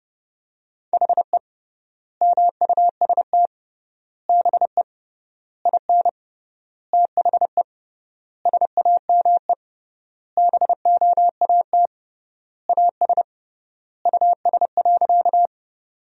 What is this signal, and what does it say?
Morse code, heard as a keyed tone, about 32 wpm, 700 Hz. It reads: HE MUST BE IN THE SAME BOAT AS US.
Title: 3x Word-Spacing - Sentences from Top 500 Words - 30wpm